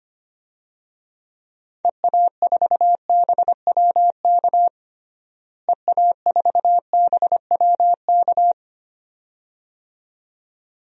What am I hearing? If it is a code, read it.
Morse code, heard as a keyed tone, about 25 wpm, 700 Hz. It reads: EA4BWK EA4BWK
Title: DXCC-100 Call Signs - 6 Characters - Repeated 2x - 25wpm